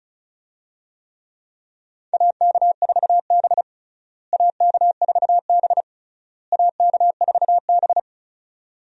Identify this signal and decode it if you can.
Morse code, heard as a keyed tone, about 34 wpm, 700 Hz. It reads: AK4B AK4B AK4B
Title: USA Call Signs - 4 Characters - Repeated 3x - 35wpm